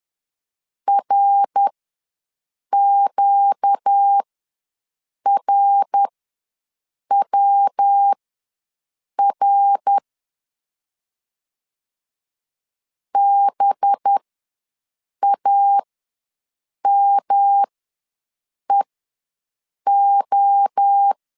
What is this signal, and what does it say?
Morse code, heard as a keyed tone, about 10 wpm, spaced as for 6 wpm, 800 Hz. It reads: RQRWR BAMEO
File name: LEZIONI / 5 LBVC / 5 LB ws 10wpm 3616